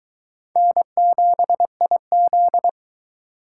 Morse code, heard as a keyed tone, about 20 wpm, 700 Hz. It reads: N7IZ